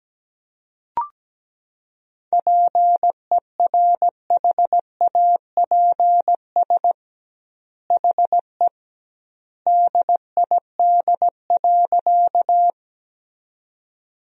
Morse code, heard as a keyed tone, about 17 wpm, 700 Hz. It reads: PERHAPS HE DID.
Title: 2x Word-Spacing - Sentences from Top 500 Words - 17wpm